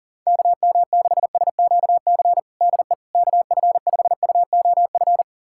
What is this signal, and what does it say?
Morse code, heard as a keyed tone, about 37 wpm, 700 Hz. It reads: KM6SQC DE KF5UOF
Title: USA Callsign DE Callsign - 6 Characters - 40wpm